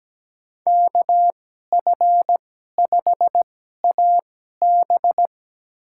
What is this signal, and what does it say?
Morse code, heard as a keyed tone, about 17 wpm, spaced as for 12 wpm, 700 Hz. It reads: KF5AB
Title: USA Call Signs - 2x Character Spacing - 17wpm